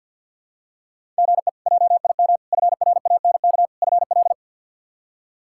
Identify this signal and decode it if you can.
Morse code, heard as a keyed tone, about 46 wpm, 700 Hz. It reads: GE JIM FRANK FL
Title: K1USN SST QSO Practice - 50wpm